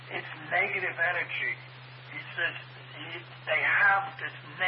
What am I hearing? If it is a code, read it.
Emergency Alert System (SAME) tones, heard as harsh buzzing bursts, with speech. It is heard from 0.5 to 4.0 s.